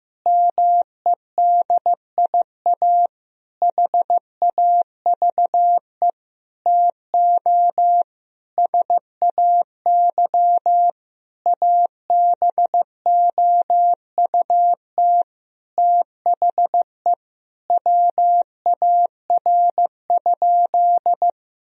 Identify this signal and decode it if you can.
Morse code, heard as a keyed tone, about 15 wpm, 700 Hz. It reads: MEDIA HAVE TO SAY ABOUT THE WAR?
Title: Sentences from Top 1000 Words - Review Part 4 - 15wpm